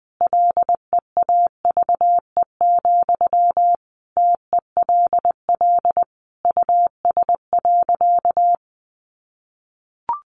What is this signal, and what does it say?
Morse code, heard as a keyed tone, about 20 wpm, 700 Hz. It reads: LEAVE, TELL US.
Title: Sentences from Top 400 Words 20wpm